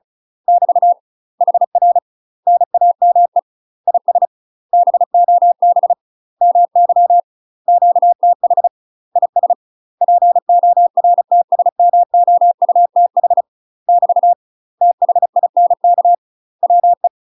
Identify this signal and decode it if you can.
Morse code, heard as a keyed tone, about 34 wpm, 700 Hz. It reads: <BT> HR NAME IS BOB MY QTH IS PORTSMOUTH <BT> THINK WE
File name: UK MASTER QSOs TRAINING / 35WPM / UK MASTER QSOs 0024